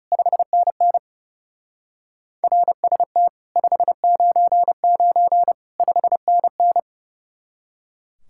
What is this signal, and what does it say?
Morse code, heard as a keyed tone, about 36 wpm, 700 Hz. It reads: 5NN RST 599 5NN
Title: Speed-Racing Common QSO Phrases - 30wpm